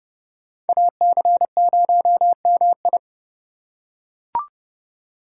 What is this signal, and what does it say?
Morse code, heard as a keyed tone, about 28 wpm, 700 Hz. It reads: AC0MI